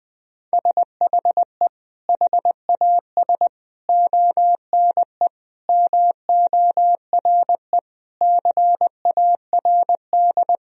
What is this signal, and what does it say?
Morse code, heard as a keyed tone, about 21 wpm, 700 Hz. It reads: SHE HAS ONE MORE CARD
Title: Sentences from Top 1000 Words - Review Part 4 - 20wpm